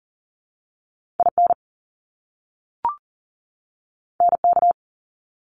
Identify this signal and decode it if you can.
Morse code, heard as a keyed tone, about 40 wpm, 700 Hz. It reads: IN NK
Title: Top Two Letter Combinations in Words 40wpm